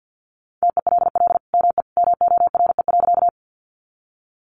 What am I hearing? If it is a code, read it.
Morse code, heard as a keyed tone, about 50 wpm, 700 Hz. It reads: TELL ME MORE.
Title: Sentences from Top 300 Words 50wpm